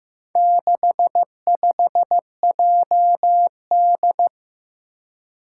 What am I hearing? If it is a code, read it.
Morse code, heard as a keyed tone, about 15 wpm, 700 Hz. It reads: 65JD
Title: Four Challenging Letter-Number Combinations 15wpm